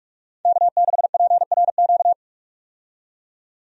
Morse code, heard as a keyed tone, about 41 wpm, 700 Hz. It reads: K6PRQ
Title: USA Call Signs - 5 Characters - 45wpm